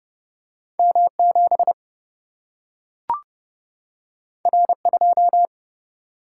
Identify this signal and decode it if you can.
Morse code, heard as a keyed tone, about 30 wpm, 700 Hz. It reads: M 7 R 2
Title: Single-Letter Single-Number 30wpm